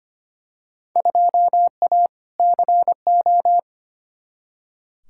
Morse code, heard as a keyed tone, about 25 wpm, 700 Hz. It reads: 2A CO
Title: Field Day Exchanges - 25wpm